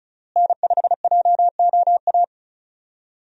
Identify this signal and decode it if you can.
Morse code, heard as a keyed tone, about 34 wpm, 700 Hz. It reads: N5JOA
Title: USA Call Signs - 5 Characters - Repeated 3x - 35wpm